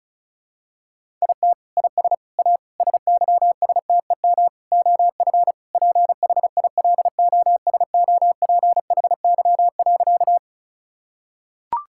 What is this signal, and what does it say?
Morse code, heard as a keyed tone, about 36 wpm, 700 Hz. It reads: IT IS A SYSTEM OF PHILOSOPHY.
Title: Sentences from Top 1000 Words - Review Part 6 - 35wpm